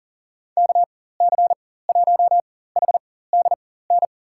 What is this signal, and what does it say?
Morse code, heard as a keyed tone, about 40 wpm, spaced as for 19 wpm, 700 Hz. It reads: KC1HDN
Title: USA Call Signs - 6 Characters - 4x Character Spacing - 40wpm